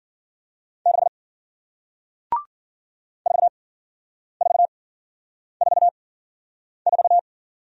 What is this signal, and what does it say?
Morse code, heard as a keyed tone, about 53 wpm, spaced as for 9 wpm, 700 Hz. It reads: 6 4444